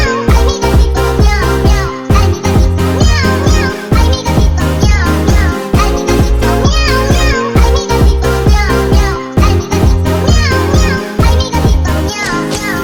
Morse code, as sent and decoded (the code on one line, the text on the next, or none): none